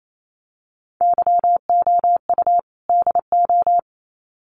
Morse code, heard as a keyed tone, about 28 wpm, 700 Hz. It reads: YOU DO
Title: Sets of 2 Words 28wpm